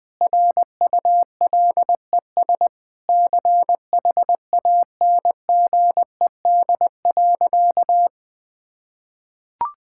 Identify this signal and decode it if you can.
Morse code, heard as a keyed tone, about 20 wpm, 700 Hz. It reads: RULES CHANGED.
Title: Sentences with 7-Word Phrases - 20wpm